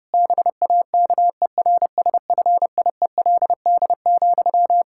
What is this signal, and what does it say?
Morse code, heard as a keyed tone, about 30 wpm, 700 Hz. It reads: BAKERSFIELD,
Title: Practice QSO - Exchange 1 - Beginning - 30wpm